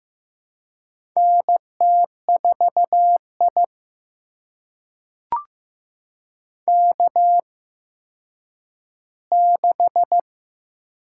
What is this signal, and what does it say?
Morse code, heard as a keyed tone, about 15 wpm, spaced as for 9 wpm, 700 Hz. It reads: NT4I K6
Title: USA Call Signs - 4 Characters - 8x Character Spacing - 15wpm